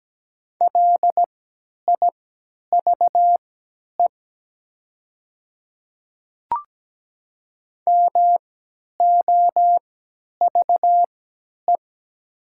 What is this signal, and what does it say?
Morse code, heard as a keyed tone, about 17 wpm, spaced as for 10 wpm, 700 Hz. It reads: LIVE MOVE
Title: 3x Character Spacing - Top 500 Words - 17wpm